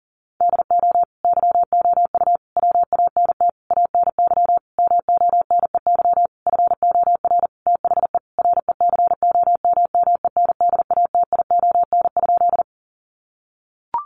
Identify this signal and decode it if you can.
Morse code, heard as a keyed tone, about 40 wpm, 700 Hz. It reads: DO YOU WANT ANY MONEY FOR THE RECOMMENDATION?
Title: Sentences from Top 1400 Words 40wpm